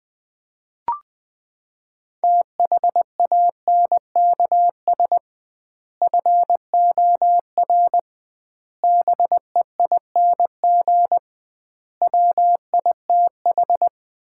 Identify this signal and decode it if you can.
Morse code, heard as a keyed tone, about 20 wpm, 700 Hz. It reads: THANKS FOR BEING WITH
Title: 2x Word-Spacing - Sets of 5 Words - 20wpm